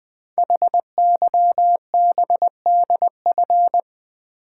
Morse code, heard as a keyed tone, about 20 wpm, 700 Hz. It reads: HYBDF
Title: Code Groups - 5 Letter:Numbers - 20wpm